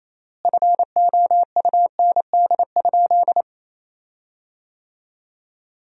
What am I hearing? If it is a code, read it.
Morse code, heard as a keyed tone, about 28 wpm, 700 Hz. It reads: FOUND?
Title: 3x Word-Spacing - Sentences from Top 300 Words - 28wpm